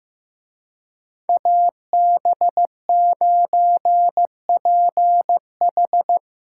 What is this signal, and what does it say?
Morse code, heard as a keyed tone, about 17 wpm, 700 Hz. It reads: AB9PH
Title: Speed Racing - USA 2x2 Call Signs 15wpm